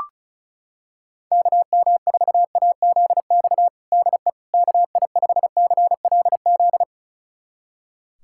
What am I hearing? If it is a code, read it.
Morse code, heard as a keyed tone, about 36 wpm, 700 Hz. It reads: KM4AZX DE KI5CLZ
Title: USA Callsign DE Callsign - 6 Characters - 35wpm